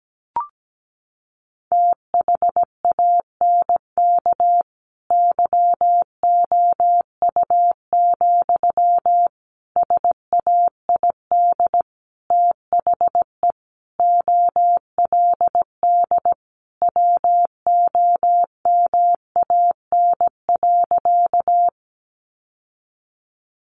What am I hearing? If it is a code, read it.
Morse code, heard as a keyed tone, about 17 wpm, 700 Hz. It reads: THANK YOU, SAID THE OLD WOMAN.
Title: Sentences from Top 1100 Words 17wpm